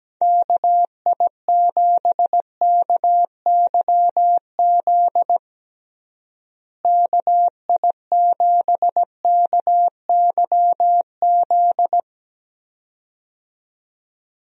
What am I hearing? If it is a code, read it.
Morse code, heard as a keyed tone, about 17 wpm, 700 Hz. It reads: KI7KYZ KI7KYZ
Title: USA Call Signs - 6 Characters - Repeated 2x - 17wpm